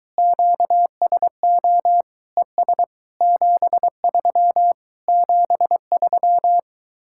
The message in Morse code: QSO ES 73 73

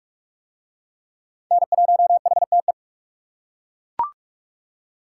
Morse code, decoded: N1HTE

